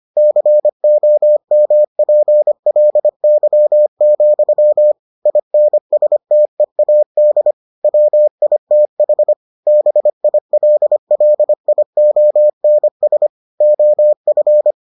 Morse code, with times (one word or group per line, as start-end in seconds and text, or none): 0.2-4.9 COMPLY,
5.3-7.5 INSTEAD
7.8-9.3 WITH
9.7-13.3 BILLIONS
13.6-14.7 OF